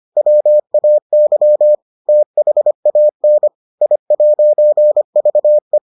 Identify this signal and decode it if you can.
Morse code, heard as a keyed tone, about 25 wpm, 600 Hz. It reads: WAY THAN I'VE